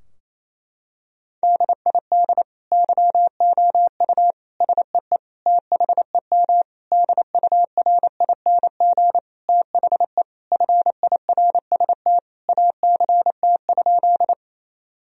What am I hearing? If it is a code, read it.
Morse code, heard as a keyed tone, about 28 wpm, 700 Hz. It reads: DID YOU SEE THEM DURING THE FIRST ACT?
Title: Sentences from Top 1000 Words - Review Part 2 - 28wpm